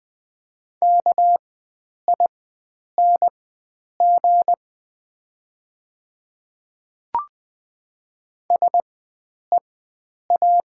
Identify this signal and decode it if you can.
Morse code, heard as a keyed tone, about 20 wpm, spaced as for 9 wpm, 690 Hz. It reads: KING SEA
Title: 4x Character Spacing - Top 500 Words - 20wpm